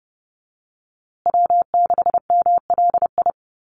Morse code, heard as a keyed tone, about 30 wpm, 700 Hz. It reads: W6MLI
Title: USA Call Signs Repeated 2x 30wpm